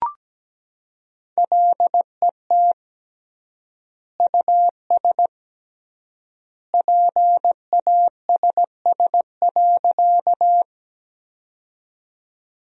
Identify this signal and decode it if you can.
Morse code, heard as a keyed tone, about 17 wpm, 700 Hz. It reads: LET US PASS.
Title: 3x Word-Spacing - Sentences from Top 400 Words - 17wpm